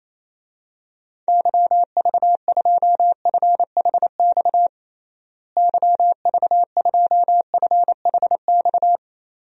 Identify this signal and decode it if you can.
Morse code, heard as a keyed tone, about 28 wpm, 700 Hz. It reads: YV2FHX YV2FHX
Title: DXCC-100 Call Signs - 6 Characters - Repeated 2x - 28wpm